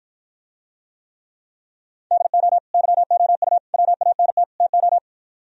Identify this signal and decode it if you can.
Morse code, heard as a keyed tone, about 54 wpm, 700 Hz. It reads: DO YOU WANT TO